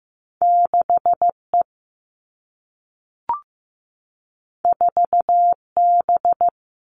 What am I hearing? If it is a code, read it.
Morse code, heard as a keyed tone, about 15 wpm, 700 Hz. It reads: 6E 4B
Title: All Cut Numbers - 2-Digit 15wpm